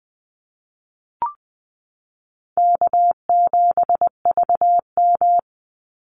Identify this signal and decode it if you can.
Morse code, heard as a keyed tone, about 20 wpm, 700 Hz. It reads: K7VM
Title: Call Signs Repeated 2x - 20wpm